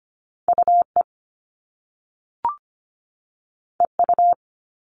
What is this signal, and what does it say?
Morse code, heard as a keyed tone, about 25 wpm, 700 Hz. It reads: UE EU
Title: All Cut Numbers - 2-Digit 25wpm